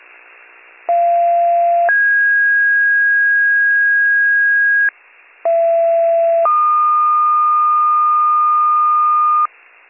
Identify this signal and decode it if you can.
Two-tone page, heard as a one-second tone, then a three-second tone, about 690 then 1650 Hz, about 670 then 1220 Hz, with noise. It is heard from 0.9 to 9.5 s.